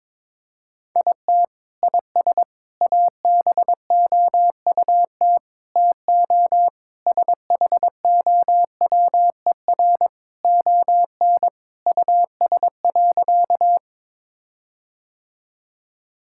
Morse code, heard as a keyed tone, about 22 wpm, 700 Hz. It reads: IT IS ABOUT TO SHOWER ON US.
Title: Sentences from Top 1600 Words 22wpm